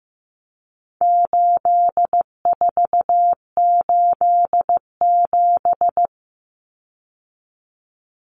Morse code, heard as a keyed tone, about 15 wpm, 700 Hz. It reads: 8487
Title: Common Cut Numbers - 4-Digit 15wpm